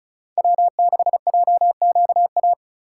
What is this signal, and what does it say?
Morse code, heard as a keyed tone, about 36 wpm, 700 Hz. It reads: W6JQA
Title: USA Call Signs - 5 Characters - 2x Character Spacing - 35wpm